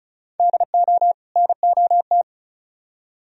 Morse code, heard as a keyed tone, about 43 wpm, 700 Hz. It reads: DO NOT
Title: Speed-Racing - Sets of 2 Words 35wpm